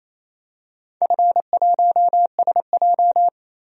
Morse code, heard as a keyed tone, about 28 wpm, 700 Hz. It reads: F1SJ
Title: DXCC-100 Call Signs - 4 Characters - 28wpm